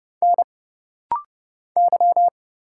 Morse code, heard as a keyed tone, about 31 wpm, 700 Hz. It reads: N Y
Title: Single Letter-Number Alt - Rapid-Fire 30wpm